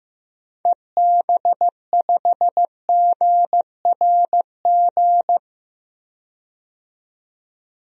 Morse code, heard as a keyed tone, about 15 wpm, 700 Hz. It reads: EB5GRG
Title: DXCC-100 Call Signs - 6 Characters - 15wpm